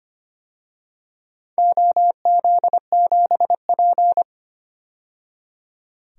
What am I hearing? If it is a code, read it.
Morse code, heard as a keyed tone, about 25 wpm, 700 Hz. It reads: OZ7P
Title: DXCC-100 Call Signs - 4 Characters - 25wpm